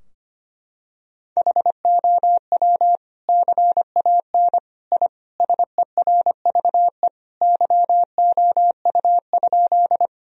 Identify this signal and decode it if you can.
Morse code, heard as a keyed tone, about 25 wpm, 700 Hz. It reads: HOW CAN I SERVE YOU?
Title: Sentences from Top 1000 Words - Review Part 2 - 25wpm